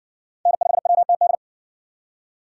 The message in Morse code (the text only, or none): N5WTD